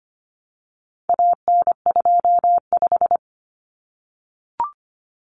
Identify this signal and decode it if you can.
Morse code, heard as a keyed tone, about 25 wpm, 700 Hz. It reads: AN25
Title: Common Cut Numbers - 4-Digit 25wpm